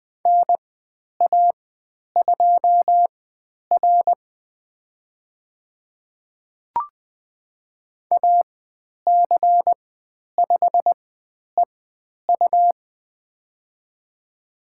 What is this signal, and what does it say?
Morse code, heard as a keyed tone, about 20 wpm, spaced as for 10 wpm, 700 Hz. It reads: NA2R AC5EU
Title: 102 Lesson 17g - Letter u - Call Signs 20wpm